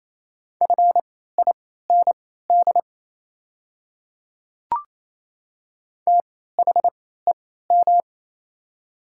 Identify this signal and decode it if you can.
Morse code, heard as a keyed tone, about 28 wpm, spaced as for 16 wpm, 700 Hz. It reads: FIND THEM
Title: 3x Character Spacing - Top 100 Words - 28wpm